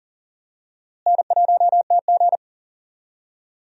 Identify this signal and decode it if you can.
Morse code, heard as a keyed tone, about 37 wpm, 700 Hz. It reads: N1TG
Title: USA Call Signs - 4 Characters - Repeated 3x - 40wpm